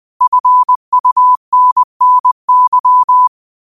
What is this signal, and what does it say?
Morse code, heard as a keyed tone, about 20 wpm, 1000 Hz. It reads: FUNNY
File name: qotdfiles / 20wpm